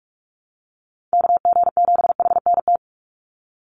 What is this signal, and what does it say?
Morse code, heard as a keyed tone, about 44 wpm, 700 Hz. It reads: KG7HNT
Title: USA Call Signs 45wpm